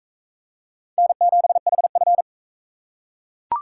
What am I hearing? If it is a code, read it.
Morse code, heard as a keyed tone, about 43 wpm, 700 Hz. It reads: N7HF